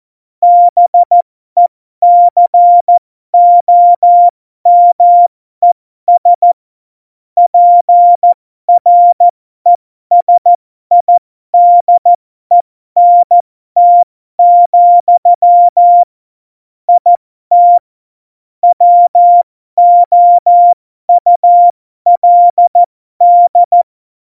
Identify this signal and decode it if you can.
Morse code, heard as a keyed tone, about 14 wpm, spaced as for 12 wpm, 700 Hz. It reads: BECOMES PRESIDENT, IT WOULD
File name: Morsekurs / Witze